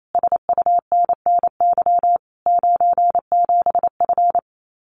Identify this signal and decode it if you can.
Morse code, heard as a keyed tone, about 28 wpm, 700 Hz. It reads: SUNNY 97F
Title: Common QSO Phrases Encore 28wpm